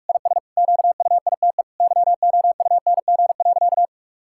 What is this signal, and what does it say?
Morse code, heard as a keyed tone, about 46 wpm, 700 Hz. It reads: IS QUITE YOUNG.
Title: Sentences from Top 1000 Words - Review Part 6 - 45wpm